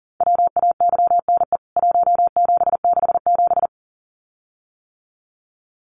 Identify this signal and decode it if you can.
Morse code, heard as a keyed tone, about 40 wpm, 700 Hz. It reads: WAYNE 1767
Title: CWT Exchange 40wpm